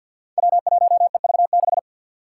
Morse code, heard as a keyed tone, about 51 wpm, 700 Hz. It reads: W1EV6
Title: Five Challenging Letter-Number Combinations 50wpm